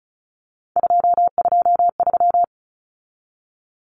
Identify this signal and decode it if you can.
Morse code, heard as a keyed tone, about 34 wpm, 700 Hz. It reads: 223
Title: Common Cut Numbers - 3-Digit 35wpm